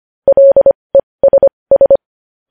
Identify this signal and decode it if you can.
Morse code, heard as a keyed tone, about 25 wpm, spaced as for 20 wpm, 550 Hz. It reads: LESS
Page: news headlines in morse code at 20 wpm